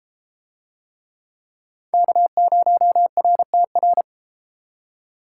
Morse code, heard as a keyed tone, about 29 wpm, 700 Hz. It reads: K0RTR